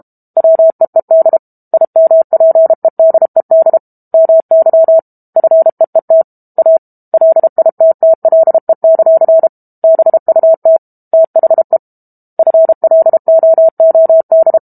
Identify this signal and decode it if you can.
Morse code, heard as a keyed tone, about 32 wpm, 650 Hz. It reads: WEED IMPEDED MY FEET A LITTLE; BUT THE FLOOD
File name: Book Two / 06-Chapter 6 - 32 WPM